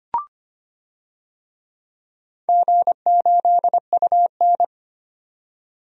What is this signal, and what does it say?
Morse code, heard as a keyed tone, about 25 wpm, 700 Hz. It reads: G8UN
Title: DXCC-100 Call Signs - 4 Characters - 25wpm